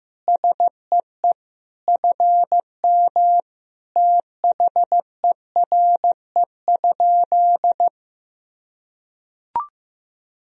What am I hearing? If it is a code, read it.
Morse code, heard as a keyed tone, about 16 wpm, 700 Hz. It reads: SEE FM THERE?
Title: Sentences from Top 100 Words and Abbreviations 15wpm